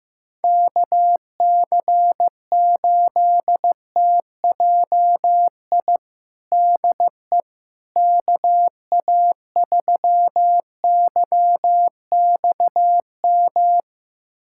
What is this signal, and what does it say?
Morse code, heard as a keyed tone, about 15 wpm, 700 Hz. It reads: KC8TJI DE KA3YXM
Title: USA Callsign DE Callsign - 6 Characters - 15wpm